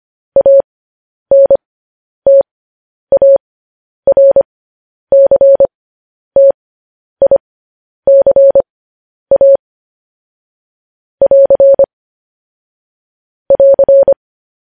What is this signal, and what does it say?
Morse code, heard as a keyed tone, about 25 wpm, spaced as for 10 wpm, 550 Hz. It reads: ANTARCTICA <AR> <AR>
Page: news headlines in morse code at 10 wpm